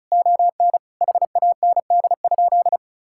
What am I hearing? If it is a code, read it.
Morse code, heard as a keyed tone, about 36 wpm, 700 Hz. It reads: ON HAND?
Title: Sentences from Top 1000 Words - Review Part 5 - 35wpm